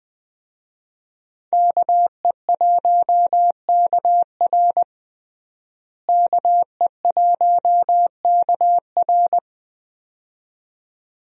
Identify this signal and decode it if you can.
Morse code, heard as a keyed tone, about 21 wpm, 700 Hz. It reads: KE1KR KE1KR
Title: USA Call Signs - 5 Characters - Repeated 2x - 20wpm